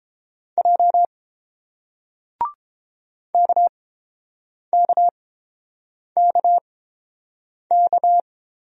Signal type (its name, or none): Morse code